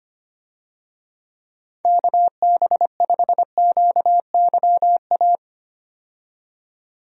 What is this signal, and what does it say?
Morse code, heard as a keyed tone, about 25 wpm, 700 Hz. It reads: KB5QYA
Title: USA Call Signs - 6 Characters - 25wpm